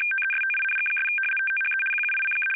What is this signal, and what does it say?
RTTY, about 75 baud. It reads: TESTING' THE QUICK
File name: Audio Files / MD-522A 75baud 850shift